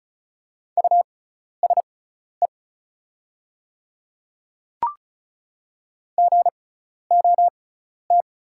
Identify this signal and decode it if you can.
Morse code, heard as a keyed tone, about 36 wpm, spaced as for 12 wpm, 690 Hz. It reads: USE GOT
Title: 6x Character Spacing - Top 500 Words - 35wpm